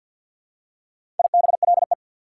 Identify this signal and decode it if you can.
Morse code, heard as a keyed tone, about 48 wpm, 690 Hz. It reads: IBLE
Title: Prefixes and Suffixes 50wpm